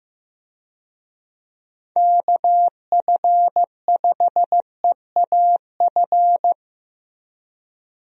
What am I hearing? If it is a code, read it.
Morse code, heard as a keyed tone, about 15 wpm, 700 Hz. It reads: KF5EAF
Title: USA Call Signs - 6 Characters - Repeated 2x - 15wpm